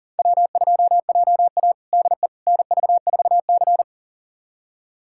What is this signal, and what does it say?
Morse code, heard as a keyed tone, about 40 wpm, 700 Hz. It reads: W2JA DE NV4C
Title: USA Callsign DE Callsign - 4 Characters - 40wpm